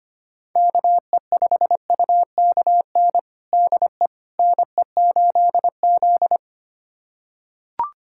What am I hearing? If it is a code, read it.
Morse code, heard as a keyed tone, about 25 wpm, 700 Hz. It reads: KE5UKN DE NE8Z
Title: USA Callsign DE Callsign 25wpm